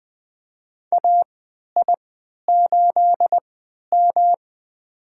Morse code, heard as a keyed tone, about 20 wpm, spaced as for 11 wpm, 700 Hz. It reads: AI8M
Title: USA Call Signs - 4 Characters - 3x Character Spacing - 20wpm